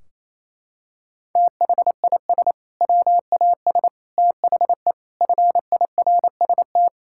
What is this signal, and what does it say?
Morse code, heard as a keyed tone, about 28 wpm, 700 Hz. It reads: THIS WAS THE FIRST
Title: Sentences from Top 1600 Words 28wpm